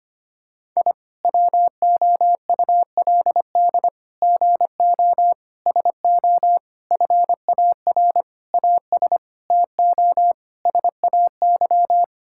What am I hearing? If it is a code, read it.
Morse code, heard as a keyed tone, about 25 wpm, 700 Hz. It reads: I WOULD GO SO FAR AS TO SAY
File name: practice / 25wpm / Sentences with 9-Word Phrases - 25wpm